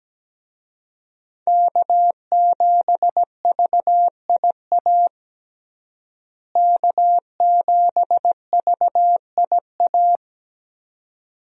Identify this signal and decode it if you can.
Morse code, heard as a keyed tone, about 17 wpm, 700 Hz. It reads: K7VIA K7VIA
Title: USA Call Signs - 5 Characters - Repeated 2x - 17wpm